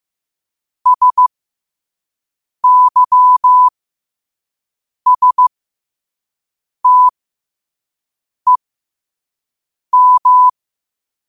Morse code, heard as a keyed tone, about 15 wpm, spaced as for 5 wpm, 1000 Hz. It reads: SYSTEM
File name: qotdfiles / 5wpm